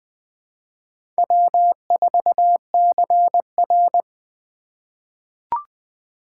Morse code, heard as a keyed tone, about 20 wpm, 700 Hz. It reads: W4CR
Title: USA Call Signs - 2x Character Spacing - 20wpm